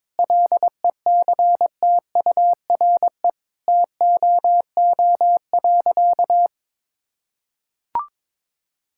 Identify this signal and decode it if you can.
Morse code, heard as a keyed tone, about 22 wpm, 700 Hz. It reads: LECTURE TOO.
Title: Sentences from Top 1600 Words 22wpm